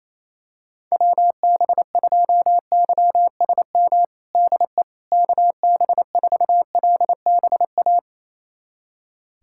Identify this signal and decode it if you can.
Morse code, heard as a keyed tone, about 28 wpm, 700 Hz. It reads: WB2YSM DE KB4LBA
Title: USA Callsign DE Callsign 28wpm